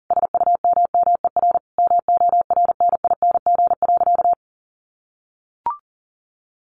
Morse code, heard as a keyed tone, about 40 wpm, 700 Hz. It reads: SUMMER MORNING.
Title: Sentences from Top 1100 Words 40wpm